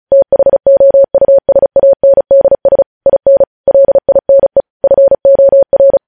Morse code, heard as a keyed tone, about 36 wpm, 550 Hz. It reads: THOUSANDS IN LINE FOR
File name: podcast / News headlines-1x-35x35